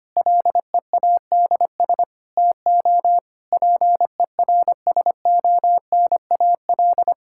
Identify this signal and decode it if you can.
Morse code, heard as a keyed tone, about 25 wpm, 700 Hz. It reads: LEADS TO PERSONAL
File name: practice / 25wpm / Difficult Sentences with Hints - 25wpm